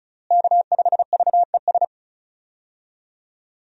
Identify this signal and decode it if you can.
Morse code, heard as a keyed tone, about 34 wpm, 700 Hz. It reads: K5VES
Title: USA Call Signs - 5 Characters - Repeated 3x - 35wpm